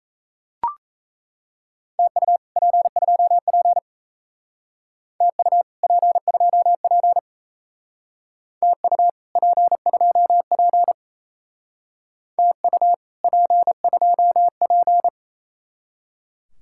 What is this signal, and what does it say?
Morse code, heard as a keyed tone, about 34 wpm, 700 Hz. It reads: TU P2P TU P2P TU P2P TU P2P
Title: Speed-Racing Common QSO Phrases Encore - 28wpm